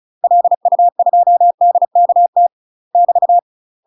Morse code, heard as a keyed tone, about 34 wpm, 700 Hz. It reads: LU2DKT <BT>
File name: UK MASTER QSOs TRAINING/35WPM